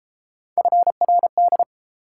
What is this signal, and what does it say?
Morse code, heard as a keyed tone, about 31 wpm, 700 Hz. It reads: FRD